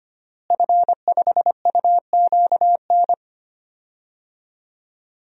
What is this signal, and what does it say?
Morse code, heard as a keyed tone, about 25 wpm, 700 Hz. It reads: F5UQN
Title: DXCC-100 Call Signs - 5 Characters - Repeated 3x - 25wpm